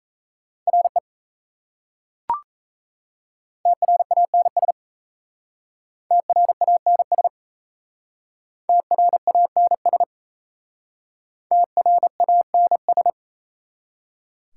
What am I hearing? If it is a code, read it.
Morse code, heard as a keyed tone, about 35 wpm, 690 Hz. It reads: RE TRANS TRANS TRANS TRANS